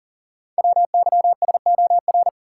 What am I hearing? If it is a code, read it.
Morse code, heard as a keyed tone, about 39 wpm, 700 Hz. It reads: WYSOR
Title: Code Groups - 5 Letters - 40wpm